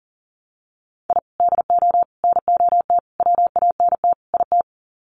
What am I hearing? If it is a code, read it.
Morse code, heard as a keyed tone, about 40 wpm, 700 Hz. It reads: I DO NOT WANT IT
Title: Sets of 5 Words Encore2 40wpm